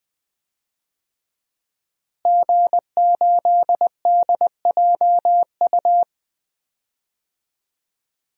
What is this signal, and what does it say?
Morse code, heard as a keyed tone, about 20 wpm, 700 Hz. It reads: G8DJU
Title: DXCC-100 Call Signs - 5 Characters - Repeated 3x - 20wpm